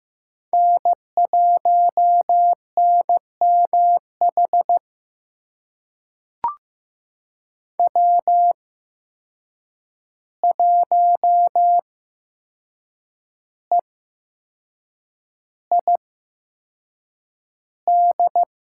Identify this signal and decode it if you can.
Morse code, heard as a keyed tone, about 15 wpm, spaced as for 6 wpm, 700 Hz. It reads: N1NMH W1EID
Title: USA Call Signs - 5 Characters - 8x Character Spacing - 15wpm